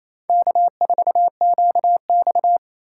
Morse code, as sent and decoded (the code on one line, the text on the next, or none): -.- ....- --.- -..-
K4QX